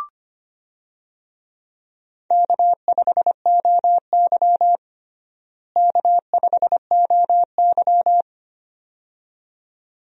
Morse code, heard as a keyed tone, about 25 wpm, 700 Hz. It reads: K5OY K5OY
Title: USA Call Signs - 4 Characters - Repeated 2x - 25wpm